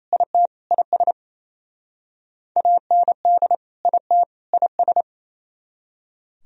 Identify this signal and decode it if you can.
Morse code, heard as a keyed tone, about 35 wpm, 700 Hz. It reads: IT IS AND IT IS
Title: Speed-Racing - Sets of 3 Words 28wpm